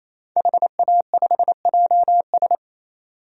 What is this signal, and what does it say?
Morse code, heard as a keyed tone, about 28 wpm, 700 Hz. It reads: HA5JS
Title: DXCC-100 Call Signs - 5 Characters - 28wpm